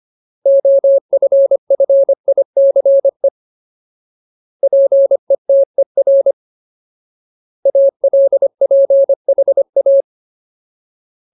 Morse code, heard as a keyed tone, about 25 wpm, 550 Hz. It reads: OFFICE PETER ALPHA
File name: Ultimate Word List / 25WPM Ultimate